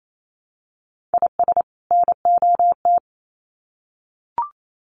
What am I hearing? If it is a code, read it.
Morse code, heard as a keyed tone, about 28 wpm, 700 Hz. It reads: IS NOT
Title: Sets of 2 Words 28wpm